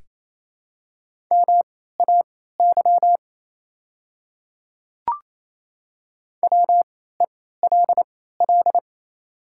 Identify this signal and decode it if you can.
Morse code, heard as a keyed tone, about 28 wpm, spaced as for 16 wpm, 700 Hz. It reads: MAY WELL
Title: 3x Character Spacing - Top 200 Words - 28wpm